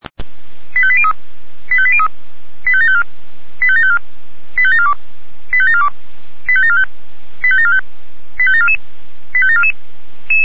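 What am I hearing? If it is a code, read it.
Five-tone selective call (ZVEI), heard as quick runs of five tones, tones of about 70 ms. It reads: 85803 85803 85754 85754 85742 85742 85745 85745 85744 85744
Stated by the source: Five-tone ZVEI selective call received on SDR.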